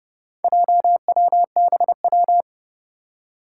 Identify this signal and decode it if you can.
Morse code, heard as a keyed tone, about 29 wpm, 700 Hz. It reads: JWBW
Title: Four Challenging Letter-Number Combinations 30wpm